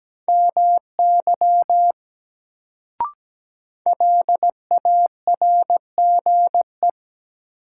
Morse code, heard as a keyed tone, about 17 wpm, 700 Hz. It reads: MY LARGE
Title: Sentences from Top 200 Words - New Word Drill - 17wpm